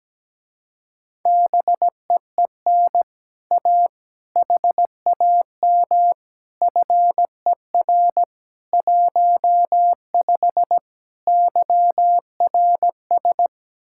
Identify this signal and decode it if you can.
Morse code, heard as a keyed tone, about 17 wpm, 700 Hz. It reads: BEEN A HAM FER 15 YRS
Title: Speed-Racing Common QSO Phrases - 17wpm